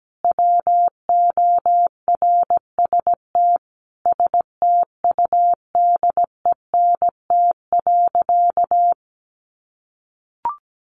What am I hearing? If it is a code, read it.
Morse code, heard as a keyed tone, about 17 wpm, 700 Hz. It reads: WORST STUDENT.
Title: Additional Easy Phrases 17wpm